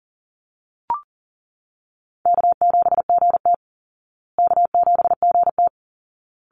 Morse code, heard as a keyed tone, about 40 wpm, 700 Hz. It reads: K7GT K7GT
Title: Call Signs Repeated 2x - 40wpm